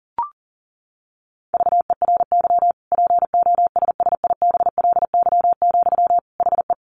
Morse code, heard as a keyed tone, about 40 wpm, 700 Hz. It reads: VERY POSSIBLY, HE
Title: Sentences from Top 1300 Words 40wpm